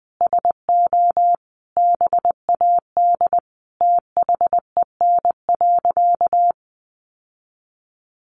Morse code, heard as a keyed tone, about 20 wpm, 700 Hz. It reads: SO BAD THEN.
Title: Sentences from Top 700 Words 20wpm